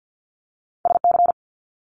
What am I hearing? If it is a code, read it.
Morse code, heard as a keyed tone, about 50 wpm, 700 Hz. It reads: SC
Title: Multi-letter Phonemes Quick Fire 50wpm